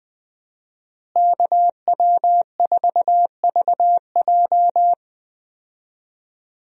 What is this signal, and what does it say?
Morse code, heard as a keyed tone, about 20 wpm, 700 Hz. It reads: KW4VJ
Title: USA Call Signs - 5 Characters - Repeated 3x - 20wpm